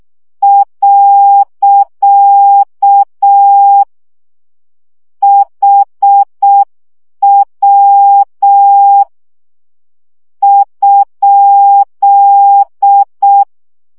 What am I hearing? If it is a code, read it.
Morse code, heard as a keyed tone, about 6 wpm, 800 Hz. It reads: . HW ?
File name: morze vizsga 2011-3 5